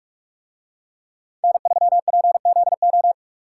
Morse code, heard as a keyed tone, about 44 wpm, 700 Hz. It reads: N3PZO